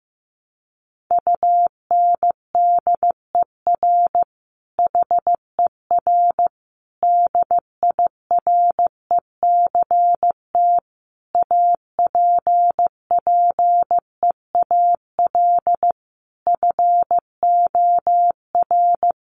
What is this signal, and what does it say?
Morse code, heard as a keyed tone, about 15 wpm, 700 Hz. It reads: UNDER HER DIRECT APPEAL FOR
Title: Sentences from Top 1300 Words 15wpm